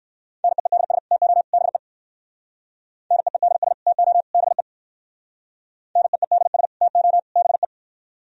Morse code, heard as a keyed tone, about 59 wpm, 700 Hz. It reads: NEEDS TO BE NEEDS TO BE NEEDS TO BE